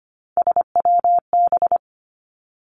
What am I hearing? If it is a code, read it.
Morse code, heard as a keyed tone, about 25 wpm, 700 Hz. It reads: SWB
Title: Three Challenging Letter-Number Combinations 25wpm